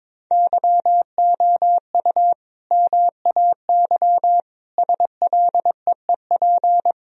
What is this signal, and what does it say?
Morse code, heard as a keyed tone, about 22 wpm, 700 Hz. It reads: YOU MAY SLEEP
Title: Sentences from Top 1000 Words - Review Part 4 - 22wpm